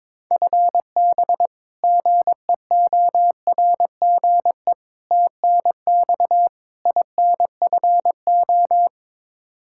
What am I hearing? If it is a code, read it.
Morse code, heard as a keyed tone, about 22 wpm, 700 Hz. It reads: FB GEORGE TNX INFO
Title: K1USN SST QSO Practice 22wpm